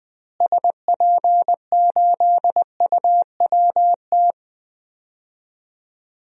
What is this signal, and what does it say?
Morse code, heard as a keyed tone, about 21 wpm, 700 Hz. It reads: SP8UWT
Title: DXCC-100 Call Signs - 6 Characters - Repeated 2x - 20wpm